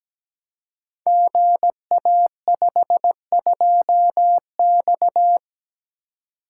Morse code, heard as a keyed tone, about 17 wpm, 700 Hz. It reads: GA52X
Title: Code Groups - 5 Letter:Numbers - 17wpm